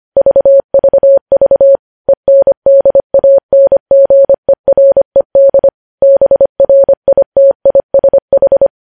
Morse code, heard as a keyed tone, about 25 wpm, 550 Hz. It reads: VVV ENDANGERED BRITISH
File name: podcast / News headlines-1x-25x25